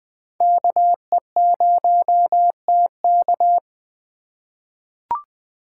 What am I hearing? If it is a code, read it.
Morse code, heard as a keyed tone, about 20 wpm, 700 Hz. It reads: KE0TK
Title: USA Call Signs - 4x Character Spacing - 20wpm